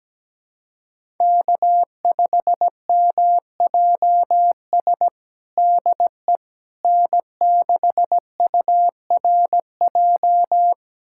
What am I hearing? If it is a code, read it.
Morse code, heard as a keyed tone, about 17 wpm, 700 Hz. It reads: K5MJS DE N6URJ
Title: USA Callsign DE Callsign - 5 Characters - 17wpm